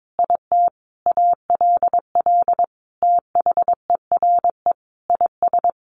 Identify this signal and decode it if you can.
Morse code, heard as a keyed tone, about 22 wpm, 700 Hz. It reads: IT ALL THERE IS
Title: Sentences from Top 1100 Words 22wpm